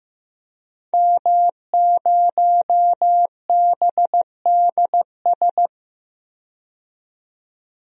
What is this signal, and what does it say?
Morse code, heard as a keyed tone, about 15 wpm, 700 Hz. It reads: M0BDS
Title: DXCC-100 Call Signs - 5 Characters - 15wpm